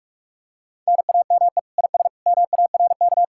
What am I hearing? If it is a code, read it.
Morse code, heard as a keyed tone, about 46 wpm, 700 Hz. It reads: NAME IS MARK
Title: Speed-Racing Common QSO Phrases - 45wpm